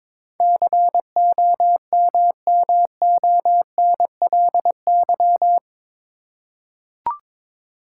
Morse code, heard as a keyed tone, about 22 wpm, 700 Hz. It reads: COMMONLY
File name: practice / 22wpm / Eight Letter Words - 22wpm